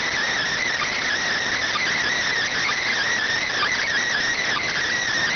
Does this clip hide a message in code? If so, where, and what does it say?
Yes: Emergency Alert System (SAME) tones (header bursts) from 0.5 to 4.8 s, heard as harsh buzzing bursts.